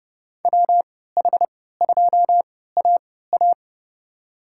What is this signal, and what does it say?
Morse code, heard as a keyed tone, about 30 wpm, spaced as for 17 wpm, 700 Hz. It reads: WH2AA
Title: USA Call Signs - 5 Characters - 3x Character Spacing - 30wpm